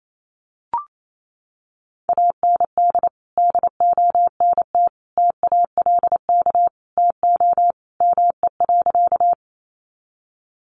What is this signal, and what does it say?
Morse code, heard as a keyed tone, about 28 wpm, 700 Hz. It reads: AND DONT TALK TO ME.
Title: Sentences from Top 400 Words 28wpm